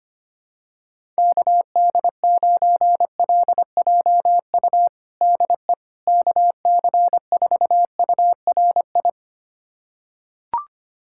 Morse code, heard as a keyed tone, about 25 wpm, 700 Hz. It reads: KD9LJU DE KC4URI
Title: USA Callsign DE Callsign 25wpm